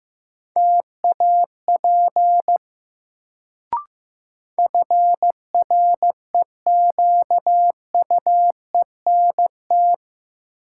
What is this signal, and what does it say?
Morse code, heard as a keyed tone, about 15 wpm, 700 Hz. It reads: TAP FREQUENT
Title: Sentences from Top 1400 Words - New Word Drill - 15wpm